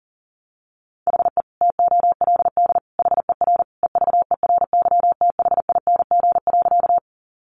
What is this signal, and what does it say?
Morse code, heard as a keyed tone, about 40 wpm, 700 Hz. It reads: HE TOLD HER EVERYTHING.
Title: Sentences from Top 700 Words 40wpm